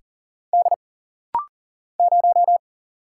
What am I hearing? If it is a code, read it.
Morse code, heard as a keyed tone, about 40 wpm, 700 Hz. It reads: D 0
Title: Single Letter-Number Alt - Rapid-Fire 40wpm